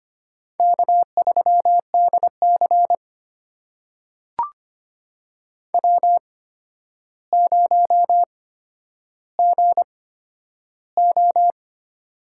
Morse code, heard as a keyed tone, about 25 wpm, spaced as for 11 wpm, 700 Hz. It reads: K3DC W0GO
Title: USA Call Signs - 4 Characters - 8x Character Spacing - 25wpm